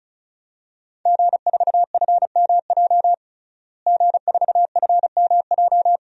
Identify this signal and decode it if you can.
Morse code, heard as a keyed tone, about 34 wpm, 700 Hz. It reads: G4FMJ G4FMJ
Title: DXCC-100 Call Signs - 5 Characters - Repeated 3x - 35wpm